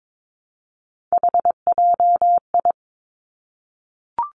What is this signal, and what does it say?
Morse code, heard as a keyed tone, about 22 wpm, 700 Hz. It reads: HJI
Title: Three Challenging Letter-Number Combinations 22wpm